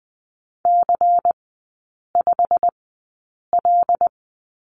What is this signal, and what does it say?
Morse code, heard as a keyed tone, about 20 wpm, spaced as for 8 wpm, 700 Hz. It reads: C5L
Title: 093 Lesson 16e - Letter c - Three Characters 20wpm